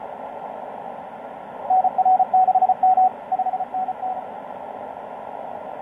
Morse code, heard as a keyed tone, about 32 wpm, 750 Hz. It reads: NR6M 5NN
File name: tx5k dl4ny etc 12m cw 04mar2013 1521